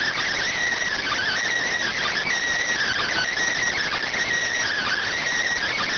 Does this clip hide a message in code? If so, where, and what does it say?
Yes: Emergency Alert System (SAME) tones from 1.0 to 5.1 s, heard as harsh buzzing bursts.